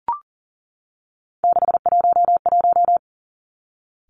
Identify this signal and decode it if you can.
Morse code, heard as a keyed tone, about 42 wpm, 690 Hz. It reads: 611
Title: Three Challenging Letter-Number Combinations 40wpm